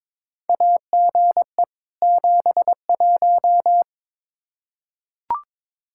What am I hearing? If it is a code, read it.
Morse code, heard as a keyed tone, about 21 wpm, 700 Hz. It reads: AGE 71
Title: Practice QSO - Exchange 3 - More Info - 22wpm